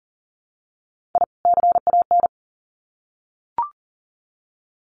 Morse code, heard as a keyed tone, about 40 wpm, 700 Hz. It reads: I CAN
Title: Sets of 2 Words 40wpm